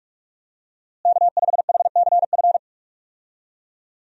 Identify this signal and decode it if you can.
Morse code, heard as a keyed tone, about 46 wpm, 700 Hz. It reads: K5HCF
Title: USA Call Signs - 5 Characters - Repeated 3x - 45wpm